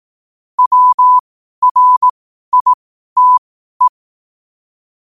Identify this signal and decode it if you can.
Morse code, heard as a keyed tone, about 18 wpm, spaced as for 12 wpm, 1000 Hz. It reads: WRITE
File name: qotdfiles / 12wpm